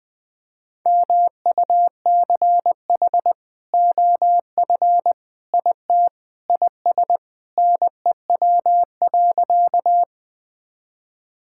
Morse code, heard as a keyed tone, about 20 wpm, 700 Hz. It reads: MUCH OF IT IS NEW.
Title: Sentences with 2-Word Phrases - 20wpm